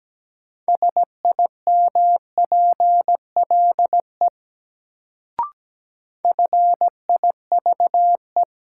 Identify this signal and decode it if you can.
Morse code, heard as a keyed tone, about 17 wpm, 700 Hz. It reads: SIMPLE FIVE
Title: Sentences from Top 400 Words - New Word Drill - 17wpm